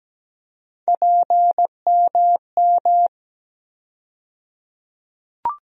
Morse code, heard as a keyed tone, about 17 wpm, 700 Hz. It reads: PMM
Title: USA Call Sign Postfixes - 3 Letters 17wpm